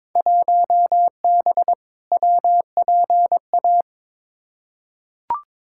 Morse code, heard as a keyed tone, about 22 wpm, 700 Hz. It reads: 1B WPA
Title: Field Day Exchanges - 22wpm